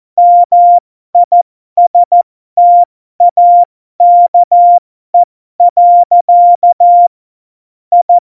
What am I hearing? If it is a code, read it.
Morse code, heard as a keyed tone, about 14 wpm, spaced as for 12 wpm, 700 Hz. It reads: MISTAKE. I
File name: Morsekurs / Witze